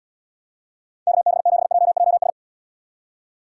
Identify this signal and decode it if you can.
Morse code, heard as a keyed tone, about 78 wpm, 700 Hz. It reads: KB7JJS